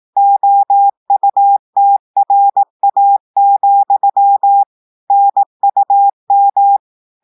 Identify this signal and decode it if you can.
Morse code, heard as a keyed tone, about 18 wpm, 800 Hz. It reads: OUTRA, NUM
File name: audios / portugais / portugais texte en clair 1 vitesse 18 mots minutes